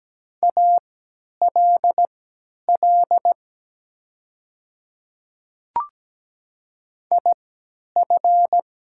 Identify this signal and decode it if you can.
Morse code, heard as a keyed tone, about 17 wpm, spaced as for 10 wpm, 700 Hz. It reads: ALL IF
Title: 3x Character Spacing - Top 200 Words - 17wpm